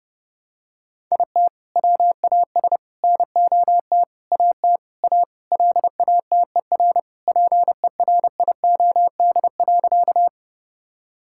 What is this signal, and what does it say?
Morse code, heard as a keyed tone, about 30 wpm, 700 Hz. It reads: IT WAS NOT AT A LATER PERIOD.
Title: Sentences from Top 1000 Words - Review Part 5 - 30wpm